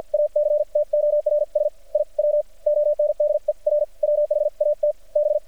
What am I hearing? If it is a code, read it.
Morse code, heard as a keyed tone, about 50 wpm, spaced as for 41 wpm, 600 Hz. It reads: ACTORS I WONDER WHAT B
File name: archived / CRSnet